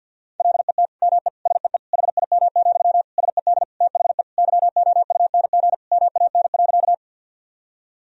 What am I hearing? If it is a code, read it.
Morse code, heard as a keyed tone, about 52 wpm, 700 Hz. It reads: LET ME SEE HIM, SED THE YOUNG MAN.
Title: Sentences from Top 400 Words and Abbreviations 50wpm